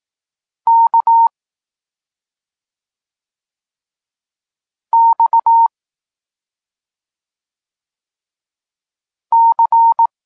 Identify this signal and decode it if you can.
Morse code, heard as a keyed tone, about 18 wpm, 920 Hz. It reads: K X C